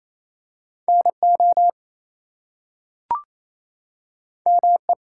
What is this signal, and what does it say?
Morse code, heard as a keyed tone, about 28 wpm, 700 Hz. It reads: NO ME
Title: Two Letter Words - 28wpm